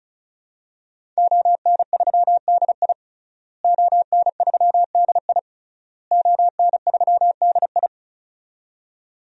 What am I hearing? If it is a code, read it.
Morse code, heard as a keyed tone, about 36 wpm, 700 Hz. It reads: ON3DI ON3DI ON3DI
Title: DXCC-100 Call Signs - 5 Characters - Repeated 3x - 35wpm